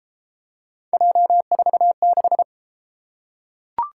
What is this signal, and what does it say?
Morse code, heard as a keyed tone, about 33 wpm, 700 Hz. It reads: J46